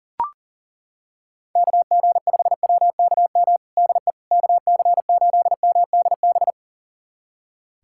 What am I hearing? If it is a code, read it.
Morse code, heard as a keyed tone, about 40 wpm, 700 Hz. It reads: KG5WKM DE KC8MDB
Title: USA Callsign DE Callsign 40wpm